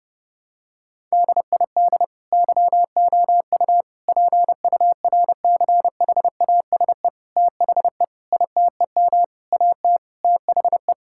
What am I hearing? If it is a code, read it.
Morse code, heard as a keyed tone, about 30 wpm, 700 Hz. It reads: DID YOU PURCHASE THE ITEM AT THE
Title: Sentences from Top 1000 Words - Review Part 6 - 30wpm